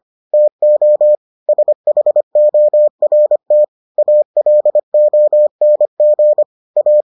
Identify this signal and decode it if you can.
Morse code, heard as a keyed tone, about 25 wpm, 600 Hz. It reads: TO SHORT ALONG A